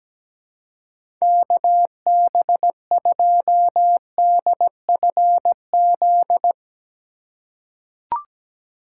Morse code, heard as a keyed tone, about 17 wpm, 700 Hz. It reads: KB2DFZ
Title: USA Call Signs - 3x Character Spacing - 17wpm